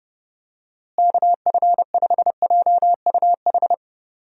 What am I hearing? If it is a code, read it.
Morse code, heard as a keyed tone, about 30 wpm, 700 Hz. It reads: KF5JUH
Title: USA Call Signs - 6 Characters - Repeated 2x - 30wpm